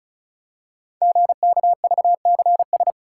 Morse code, heard as a keyed tone, about 36 wpm, 690 Hz. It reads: GKVCS
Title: Code Groups - 5 Letters - 35wpm